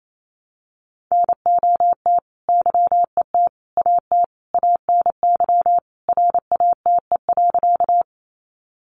Morse code, heard as a keyed tone, about 28 wpm, 700 Hz. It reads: NOT YET AT ANY RATE.
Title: Sentences from Top 700 Words 28wpm